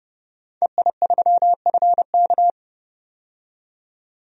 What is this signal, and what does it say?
Morse code, heard as a keyed tone, about 30 wpm, 700 Hz. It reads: EI3FK
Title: DXCC-100 Call Signs - 5 Characters - 30wpm